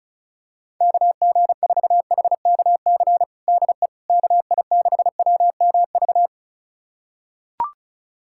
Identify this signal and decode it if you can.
Morse code, heard as a keyed tone, about 34 wpm, 700 Hz. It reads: KG4HKC DE KI6WMV
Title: USA Callsign DE Callsign 35wpm